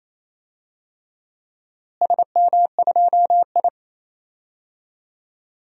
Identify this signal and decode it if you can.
Morse code, heard as a keyed tone, about 28 wpm, 700 Hz. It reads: SM2I
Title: DXCC-100 Call Signs - 4 Characters - Repeated 3x - 28wpm